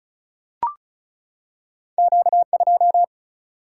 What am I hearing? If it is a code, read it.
Morse code, heard as a keyed tone, about 36 wpm, 700 Hz. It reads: Q 2
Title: Single-Letter Single-Number 35wpm